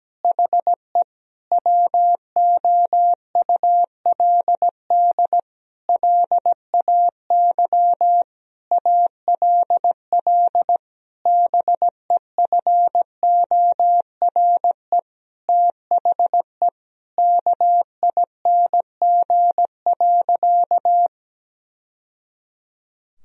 Morse code, heard as a keyed tone, about 17 wpm, 700 Hz. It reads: HE WOULD LAY ALL BEFORE THE KING.
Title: Sentences from Top 1000 Words - Review Part 2 - 17wpm